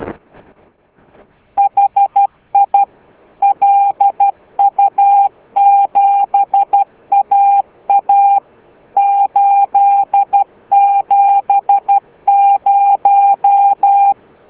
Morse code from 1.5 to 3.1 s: HI